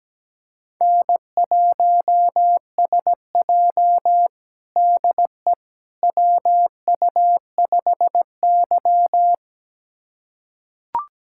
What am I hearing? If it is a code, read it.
Morse code, heard as a keyed tone, about 17 wpm, 700 Hz. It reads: N1SJ DE WU5Y
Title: USA Callsign DE Callsign - 4 Characters - 17wpm